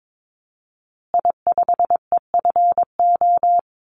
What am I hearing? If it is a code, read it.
Morse code, heard as a keyed tone, about 22 wpm, 700 Hz. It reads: I5EFO
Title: Call Signs Repeated 2x - 22wpm